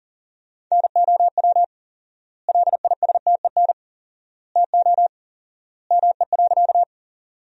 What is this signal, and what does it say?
Morse code, heard as a keyed tone, about 40 wpm, 700 Hz. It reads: NOW LISTEN TO ME.
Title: 4x Word-Spacing - Sentences from Top 400 Words - 40wpm